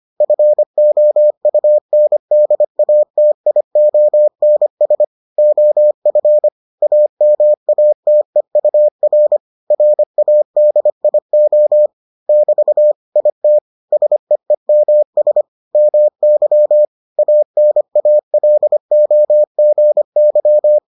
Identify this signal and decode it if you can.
Morse code, heard as a keyed tone, about 25 wpm, 600 Hz. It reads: FOUNDATIONS OF AMATEUR RADIO <BT> IT SEEMS MY ANALOGY